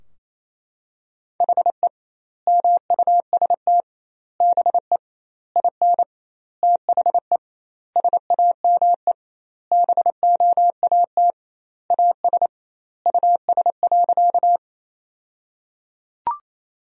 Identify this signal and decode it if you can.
Morse code, heard as a keyed tone, about 28 wpm, 700 Hz. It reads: HE MUST BE IN THE SAME BOAT AS US.
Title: 2x Word-Spacing - Sentences from Top 500 Words - 28wpm